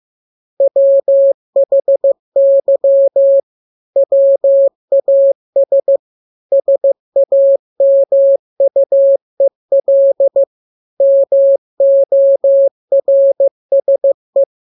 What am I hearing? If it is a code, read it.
Morse code, heard as a keyed tone, about 15 wpm, 550 Hz. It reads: WHY WAS SAMUEL MORSE
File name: jokes / joke 1 15 0000